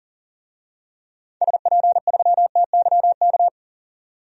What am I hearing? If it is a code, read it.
Morse code, heard as a keyed tone, about 40 wpm, 700 Hz. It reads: SP3TYK
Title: DXCC-100 Call Signs - 6 Characters - 40wpm